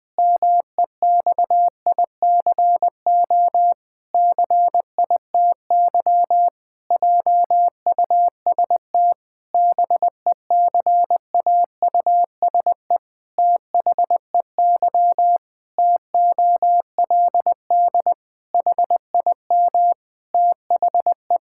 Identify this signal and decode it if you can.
Morse code, heard as a keyed tone, about 21 wpm, 700 Hz. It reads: MEXICO CITY JUST BECAUSE THEY TOLD HIM THE
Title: Pangram Challenge v2 - 20wpm